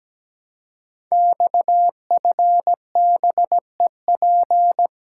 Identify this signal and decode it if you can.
Morse code, heard as a keyed tone, about 17 wpm, 700 Hz. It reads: XFBEP
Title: Code Groups - 5 Letter:Numbers - 17wpm